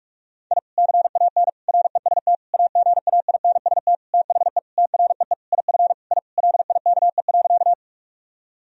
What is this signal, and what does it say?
Morse code, heard as a keyed tone, about 42 wpm, 700 Hz. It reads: I CAN REST AGAINST THE TREE IF I LIKE.
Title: Sentences from Top 1000 Words - Review Part 1 - 45wpm